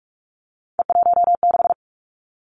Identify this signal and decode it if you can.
Morse code, heard as a keyed tone, about 46 wpm, 700 Hz. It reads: E16
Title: Three Challenging Letter-Number Combinations 45wpm